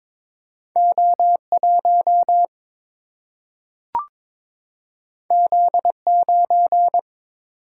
Morse code, heard as a keyed tone, about 21 wpm, 700 Hz. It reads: O1 Z9
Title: Single-Letter Single-Number 22wpm